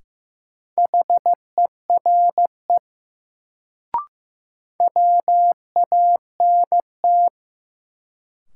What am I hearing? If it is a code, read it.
Morse code, heard as a keyed tone, about 15 wpm, 700 Hz. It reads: HERE WANT
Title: Sentences from Top 200 Words - New Word Drill - 15wpm